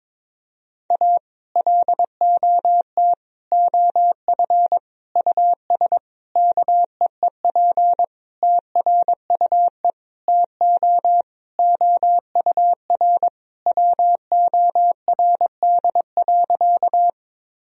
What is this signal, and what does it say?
Morse code, heard as a keyed tone, about 22 wpm, 700 Hz. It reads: A LOT OF US KEEP TRUE TO OUR WORD.
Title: Sentences from Top 1000 Words - Review Part 3 - 22wpm